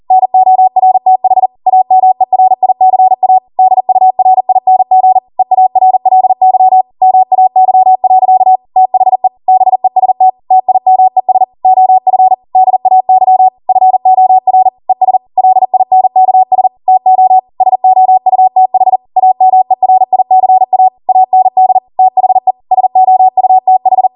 Morse code, heard as a keyed tone, about 40 wpm, 750 Hz. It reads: NORTH AMERICA DURING EARLY MAY. THE BEST TIMES OF DAY FOR ES LINKS TO SOUTH AMERICA AND THE SOUTH